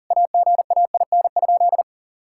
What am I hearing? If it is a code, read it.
Morse code, heard as a keyed tone, about 40 wpm, 700 Hz. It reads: AGAIN?
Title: Sentences from Top 1600 Words 40wpm